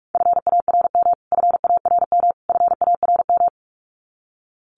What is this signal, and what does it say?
Morse code, heard as a keyed tone, about 44 wpm, 700 Hz. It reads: FARM FARM FARM
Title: Top 500 Words Repeated 45wpm